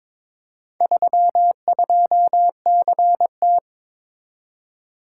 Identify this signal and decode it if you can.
Morse code, heard as a keyed tone, about 23 wpm, 700 Hz. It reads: 32CT
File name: practice / 22wpm / Four Letter-Number Combinations - 22wpm